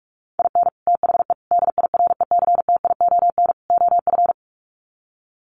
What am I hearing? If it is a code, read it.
Morse code, heard as a keyed tone, about 44 wpm, 700 Hz. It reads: IN THE DIRECTION OF
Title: Sets of 4 Words 45wpm